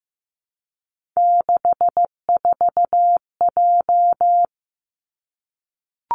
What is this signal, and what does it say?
Morse code, heard as a keyed tone, about 15 wpm, 700 Hz. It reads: 64J
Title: Three Challenging Letter-Number Combinations 15wpm